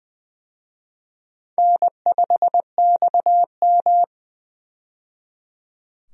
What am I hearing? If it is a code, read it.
Morse code, heard as a keyed tone, about 20 wpm, 700 Hz. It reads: N5XM
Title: USA Call Signs - 4 Characters - 20wpm